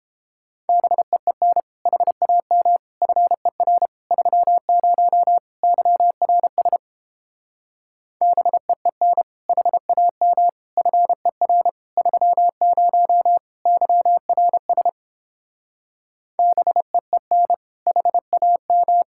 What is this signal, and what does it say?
Morse code, heard as a keyed tone, about 29 wpm, 700 Hz. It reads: BEEN HAM FER 30 YRS BEEN HAM FER 30 YRS BEEN HAM